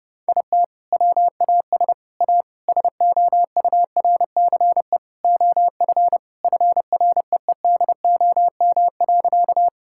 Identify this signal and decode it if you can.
Morse code, heard as a keyed tone, about 32 wpm, 700 Hz. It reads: IT WAS A SOURCE OF FREEDOM.
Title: Sentences from Top 1000 Words - Review Part 5 - 30wpm